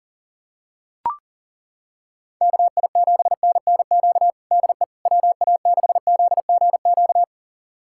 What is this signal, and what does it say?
Morse code, heard as a keyed tone, about 40 wpm, 700 Hz. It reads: KI7NNQ DE WA6ZGQ
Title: USA Callsign DE Callsign - 6 Characters - 40wpm